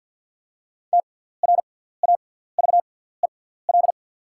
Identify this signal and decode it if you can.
Morse code, heard as a keyed tone, about 54 wpm, spaced as for 17 wpm, 690 Hz. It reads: TRAVEL
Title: 6x Character Spacing - Top 500 Words - 50wpm